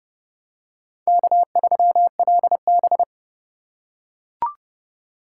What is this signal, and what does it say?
Morse code, heard as a keyed tone, about 27 wpm, 700 Hz. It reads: K3LB